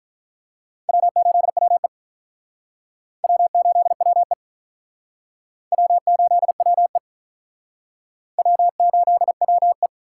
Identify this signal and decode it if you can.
Morse code, heard as a keyed tone, about 46 wpm, 700 Hz. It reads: W8WE W8WE W8WE W8WE